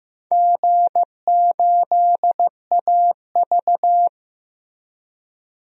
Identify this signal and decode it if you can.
Morse code, heard as a keyed tone, about 15 wpm, 700 Hz. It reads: G8AV
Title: DXCC-100 Call Signs - 4 Characters - Repeated 3x - 15wpm